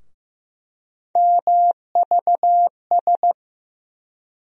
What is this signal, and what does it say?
Morse code, heard as a keyed tone, about 16 wpm, 700 Hz. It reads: MVS
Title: USA Call Sign Postfixes - 3 Letters 15wpm